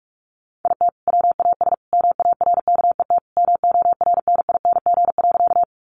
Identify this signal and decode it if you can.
Morse code, heard as a keyed tone, about 44 wpm, 700 Hz. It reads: IT WAS MARKET MORNING.
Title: Sentences from Top 600 Words 45wpm